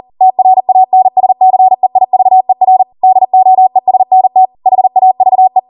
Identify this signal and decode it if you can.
Morse code, heard as a keyed tone, about 40 wpm, 750 Hz. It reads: TRANSCEIVER DOESNT HAVE